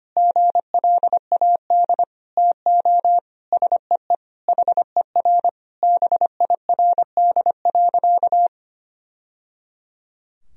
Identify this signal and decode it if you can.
Morse code, heard as a keyed tone, about 25 wpm, 700 Hz. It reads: GLAD TO SEE HER BIRD.
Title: Sentences from Top 1000 Words - Review Part 2 - 25wpm